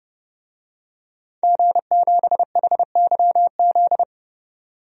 Morse code, heard as a keyed tone, about 30 wpm, 700 Hz. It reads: G7HYZ
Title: DXCC-100 Call Signs - 5 Characters - 30wpm